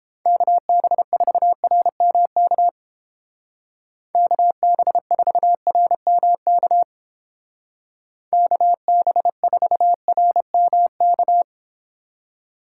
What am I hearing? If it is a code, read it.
Morse code, heard as a keyed tone, about 30 wpm, 700 Hz. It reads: KB4RMK KB4RMK KB4RMK